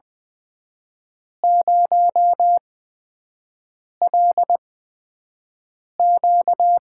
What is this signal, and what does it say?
Morse code, heard as a keyed tone, about 20 wpm, spaced as for 5 wpm, 700 Hz. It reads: 0LQ
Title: USA Call Signs - 5 Characters - 8x Character Spacing - 20wpm